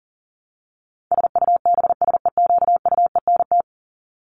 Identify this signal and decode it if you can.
Morse code, heard as a keyed tone, about 40 wpm, 700 Hz. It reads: SUBSEQUENT
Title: LY Suffix - Top 100 Words with Base Words 40wpm